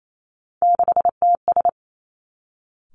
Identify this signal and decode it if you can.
Morse code, heard as a keyed tone, about 28 wpm, 700 Hz. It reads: 6TS
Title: Three Challenging Letter-Number Combinations 28wpm